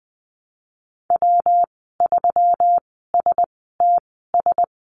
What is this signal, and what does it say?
Morse code, heard as a keyed tone, about 20 wpm, spaced as for 15 wpm, 700 Hz. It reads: W3STS
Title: 140 Lesson 23g - Letter f - Call Signs 20wpm